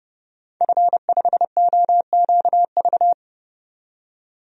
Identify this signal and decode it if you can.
Morse code, heard as a keyed tone, about 31 wpm, 700 Hz. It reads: F5OQV
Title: DXCC-100 Call Signs - 5 Characters - 30wpm